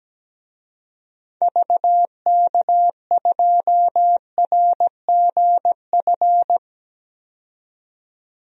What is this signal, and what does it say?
Morse code, heard as a keyed tone, about 17 wpm, 700 Hz. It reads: VK2RGF
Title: DXCC-100 Call Signs - 6 Characters - 17wpm